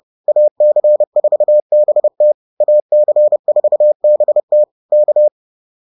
Morse code, heard as a keyed tone, about 30 wpm, 600 Hz. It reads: AC4BT AC4BT K
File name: 30wpm-7